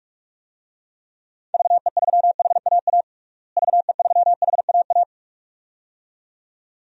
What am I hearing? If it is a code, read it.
Morse code, heard as a keyed tone, about 46 wpm, 700 Hz. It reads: VE3HAA VE3HAA
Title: DXCC-100 Call Signs - 6 Characters - Repeated 2x - 45wpm